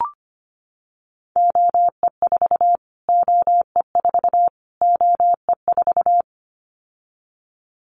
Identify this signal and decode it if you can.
Morse code, heard as a keyed tone, about 25 wpm, 700 Hz. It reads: OE4 OE4 OE4
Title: Callsign Prefixes for Top 25 DXCC - Repeated 3x - 25wpm